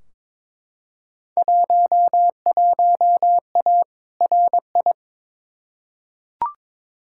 Morse code, heard as a keyed tone, about 22 wpm, 700 Hz. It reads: 11A RI
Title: Field Day Exchanges Encore - 22wpm